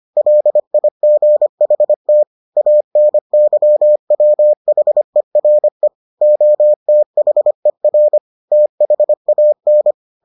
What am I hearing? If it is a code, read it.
Morse code, heard as a keyed tone, about 25 wpm, 600 Hz. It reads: LIGHT ANYWHERE OTHER THAN